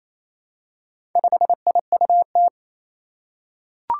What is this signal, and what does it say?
Morse code, heard as a keyed tone, about 28 wpm, 700 Hz. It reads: 5IUT
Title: Four Challenging Letter-Number Combinations 28wpm